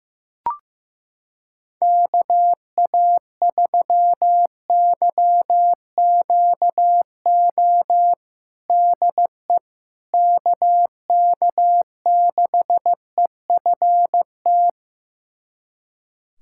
Morse code, heard as a keyed tone, about 15 wpm, 700 Hz. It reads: KA3YQO DE KK6EFT
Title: USA Callsign DE Callsign - 6 Characters - 15wpm